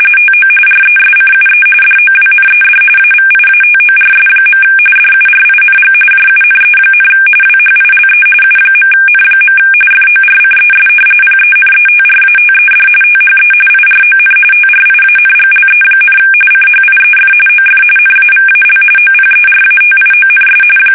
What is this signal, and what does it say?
RTTY, about 45.45 baud. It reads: TO RESCUE DOWNED AVIATORS. DURING JUNE, WHILE U.N. FORCES PENETRATED THE IRON TRIANGLE, CHIEF OF NAVAL OPERATIONS
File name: fsk850 / battleshipnj